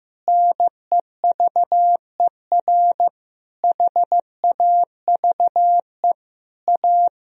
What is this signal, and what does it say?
Morse code, heard as a keyed tone, about 15 wpm, 700 Hz. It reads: NEVER HAVE A
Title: Sentences from Top 1000 Words - Review Part 2 - 15wpm